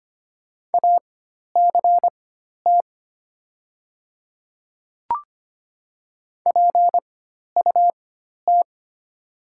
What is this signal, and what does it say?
Morse code, heard as a keyed tone, about 25 wpm, spaced as for 12 wpm, 700 Hz. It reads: ACT PUT
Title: 4x Character Spacing - Top 500 Words - 25wpm